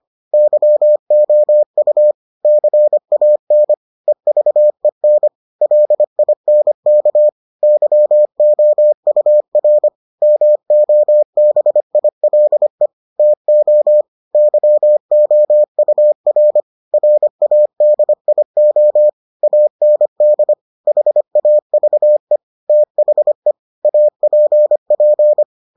Morse code, heard as a keyed tone, about 25 wpm, 600 Hz. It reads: YOU CAN EVEN LINK YOUR MOBILE TO YOUR RADIO AND HAVE THE APP